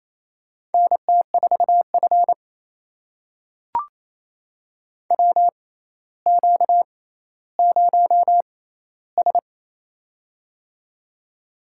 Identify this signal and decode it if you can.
Morse code, heard as a keyed tone, about 28 wpm, spaced as for 14 wpm, 700 Hz. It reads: NT4F WQ0S
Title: USA Call Signs - 4 Characters - 6x Character Spacing - 28wpm